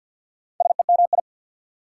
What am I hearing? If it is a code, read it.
Morse code, heard as a keyed tone, about 49 wpm, 700 Hz. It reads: SEMI
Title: Prefixes and Suffixes 50wpm